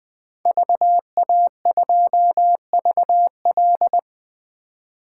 Morse code, heard as a keyed tone, about 20 wpm, 700 Hz. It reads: VA2VL
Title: 2023 SOTA Chaser Call Signs - 20wpm